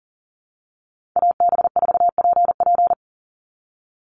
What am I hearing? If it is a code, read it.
Morse code, heard as a keyed tone, about 40 wpm, 700 Hz. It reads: AB4PP
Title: NA SOTA Chaser Callsigns 40wpm